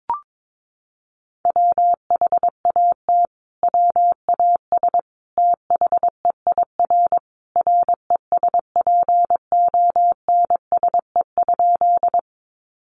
Morse code, heard as a keyed tone, about 22 wpm, 700 Hz. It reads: WHAT WAS THEIR RESPONSE?
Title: Sentences from Top 1400 Words 22wpm